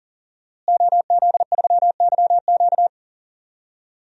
Morse code, heard as a keyed tone, about 40 wpm, 700 Hz. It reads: OZ3YQ
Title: DXCC-100 Call Signs - 5 Characters - Repeated 2x - 40wpm